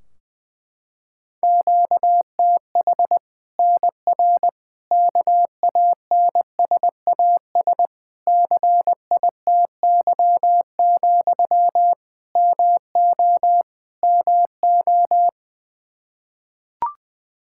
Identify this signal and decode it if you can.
Morse code, heard as a keyed tone, about 20 wpm, 700 Hz. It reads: QTH NR KANSAS CITY, MO MO
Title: Practice QSO - Exchange 1 - Beginning - 20wpm